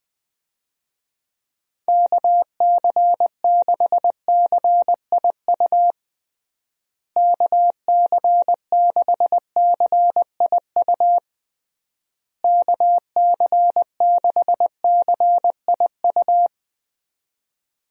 Morse code, heard as a keyed tone, about 21 wpm, 700 Hz. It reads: KC6CIU KC6CIU KC6CIU
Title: USA Call Signs - 6 Characters - Repeated 3x - 20wpm